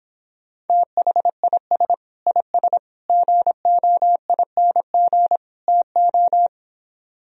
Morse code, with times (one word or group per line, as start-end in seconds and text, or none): 0.7-1.9 THIS
2.3-2.8 IS
3.1-5.4 GOING
5.7-6.5 TO